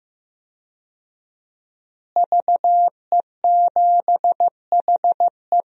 Morse code, heard as a keyed tone, about 15 wpm, 700 Hz. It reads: VE7HE
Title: DXCC-100 Call Signs - 5 Characters - Repeated 2x - 15wpm